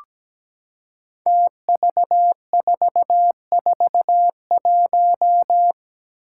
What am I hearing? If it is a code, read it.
Morse code, heard as a keyed tone, about 17 wpm, 700 Hz. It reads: TV441
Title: Five Challenging Letter-Number Combinations 17wpm